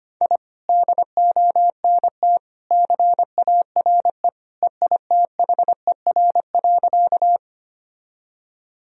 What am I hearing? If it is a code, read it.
Morse code, heard as a keyed tone, about 25 wpm, 700 Hz. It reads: I DONT CARE EITHER.
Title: Sentences from Top 1000 Words - Review Part 4 - 25wpm